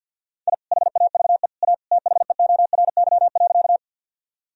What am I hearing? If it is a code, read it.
Morse code, heard as a keyed tone, about 52 wpm, 700 Hz. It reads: I HAVE A THEORY.
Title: Sentences from Top 1000 Words - Review Part 3 - 50wpm